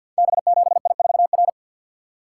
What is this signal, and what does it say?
Morse code, heard as a keyed tone, about 52 wpm, 700 Hz. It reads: B7I4R
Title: Code Groups - 5 Letter:Numbers - 50wpm